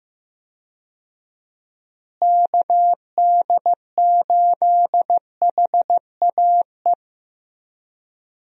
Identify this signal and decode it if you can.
Morse code, heard as a keyed tone, about 15 wpm, 700 Hz. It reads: KD8HAE
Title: USA Call Signs - 6 Characters - Repeated 3x - 15wpm